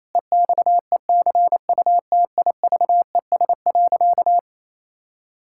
Morse code, heard as a keyed tone, about 28 wpm, 700 Hz. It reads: EXECUTIVES.
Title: Pangram Challenge v2 - 28wpm